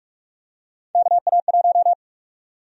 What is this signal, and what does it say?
Morse code, heard as a keyed tone, about 44 wpm, 700 Hz. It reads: KA1
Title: USA Call Sign Prefixes - 45wpm